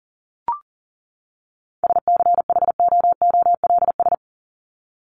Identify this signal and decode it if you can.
Morse code, heard as a keyed tone, about 40 wpm, 700 Hz. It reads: SCHOOLS
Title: ES:S Suffix - Top 100 Words with Base Words 40wpm